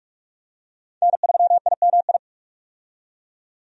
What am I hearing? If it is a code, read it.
Morse code, heard as a keyed tone, about 44 wpm, 700 Hz. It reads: N3IMI
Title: USA Call Signs - 5 Characters - 45wpm